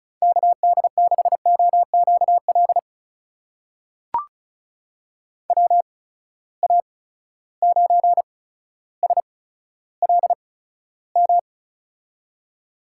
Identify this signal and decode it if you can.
Morse code, heard as a keyed tone, about 36 wpm, spaced as for 15 wpm, 700 Hz. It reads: KD6OQL WA9SLM
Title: USA Call Signs - 6 Characters - 8x Character Spacing - 35wpm